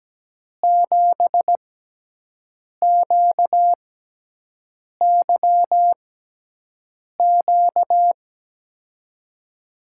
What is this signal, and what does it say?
Morse code, heard as a keyed tone, about 17 wpm, spaced as for 6 wpm, 700 Hz. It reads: 7QYQ
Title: USA Call Signs - 6 Characters - 6x Character Spacing - 17wpm